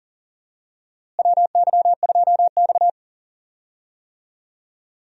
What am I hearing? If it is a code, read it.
Morse code, heard as a keyed tone, about 37 wpm, 700 Hz. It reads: WY2X
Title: USA Call Signs - 4 Characters - Repeated 2x - 40wpm